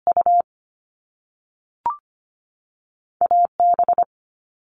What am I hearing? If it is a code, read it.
Morse code, heard as a keyed tone, about 25 wpm, 700 Hz. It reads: U AB
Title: Top Two Letter Combinations in Words 25wpm